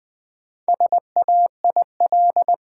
Morse code, heard as a keyed tone, about 20 wpm, 700 Hz. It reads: SAIL
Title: Build A Word - 1 Letter At A Time - 3 to 5-Letter Words - 20wpm